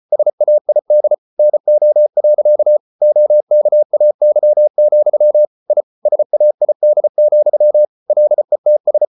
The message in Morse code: SAID NO. OKAY, I SAID, LETS

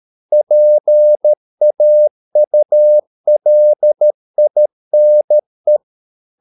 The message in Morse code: PAULINE

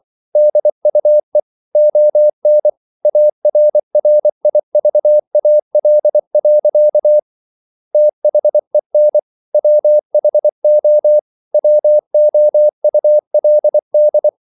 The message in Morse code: DUE ON ARRIVAL. THEN WHO WOULD

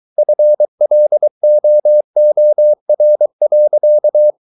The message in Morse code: FLOOR.

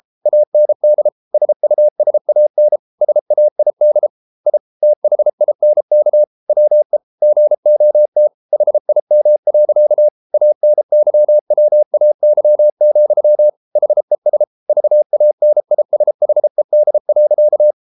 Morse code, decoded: AND SUSAN SAID I THINK WE GOT HIM. ANYWAY, HES VANISHED.

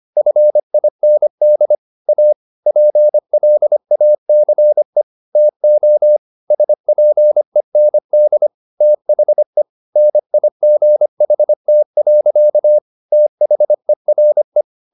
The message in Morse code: FIND A PLACE TO SPEND THE NIGHT. THERE

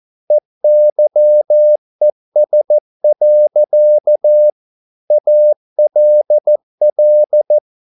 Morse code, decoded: EYES. ALL

